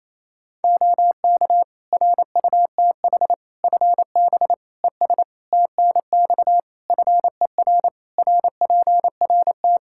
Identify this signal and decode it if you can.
Morse code, heard as a keyed tone, about 28 wpm, 700 Hz. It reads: OK RUTH FB ES TNX FER RPRT
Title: Practice QSO - Exchange 2 - Info - 28wpm